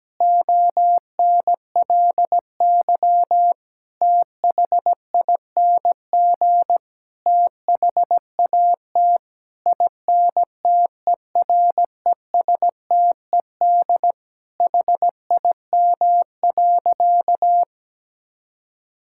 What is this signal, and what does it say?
Morse code, heard as a keyed tone, about 17 wpm, 700 Hz. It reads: ONLY THING THAT INTERESTED HIM.
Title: Sentences from Top 1000 Words - Review Part 6 - 17wpm